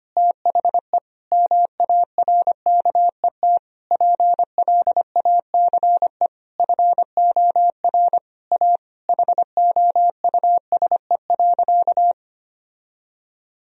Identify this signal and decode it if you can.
Morse code, heard as a keyed tone, about 25 wpm, 700 Hz. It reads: THE MARKET PLACE FOR A HOUSE.
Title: Sentences from Top 1000 Words - Review Part 3 - 25wpm